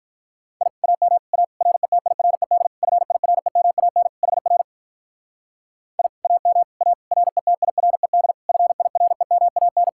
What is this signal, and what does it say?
Morse code, heard as a keyed tone, about 44 wpm, 700 Hz. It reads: I AM A RETIRED FIREMAN HR I AM A RETIRED FIREMAN
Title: Speed-Racing Common QSO Phrases Encore - 35wpm